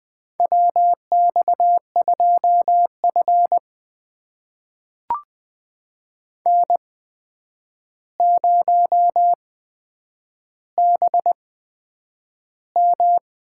Morse code, heard as a keyed tone, about 21 wpm, spaced as for 9 wpm, 700 Hz. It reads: WX2F N0BM
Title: USA Call Signs - 4 Characters - 8x Character Spacing - 20wpm